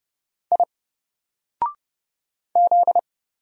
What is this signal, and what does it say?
Morse code, heard as a keyed tone, about 31 wpm, 700 Hz. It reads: I Z
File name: Single-Letter-Number-30wpm